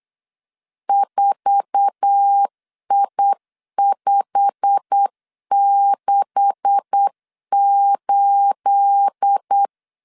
Morse code, heard as a keyed tone, about 8 wpm, 800 Hz. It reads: 4I568
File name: LEZIONI / 9 286 8wpm 3425